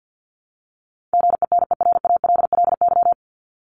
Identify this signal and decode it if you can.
Morse code, heard as a keyed tone, about 54 wpm, 700 Hz. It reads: GENERALLY
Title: LY Suffix - Top 100 Words with Base Words 50wpm